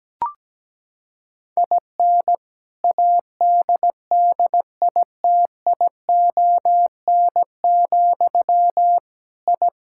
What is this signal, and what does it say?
Morse code, heard as a keyed tone, about 17 wpm, 700 Hz. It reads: IN ADDITION, I
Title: Sentences from Top 1000 Words - Review Part 6 - 17wpm